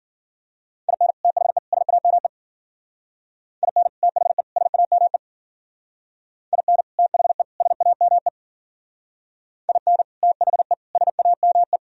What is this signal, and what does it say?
Morse code, heard as a keyed tone, about 51 wpm, 700 Hz. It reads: IN THE SAME IN THE SAME IN THE SAME IN THE SAME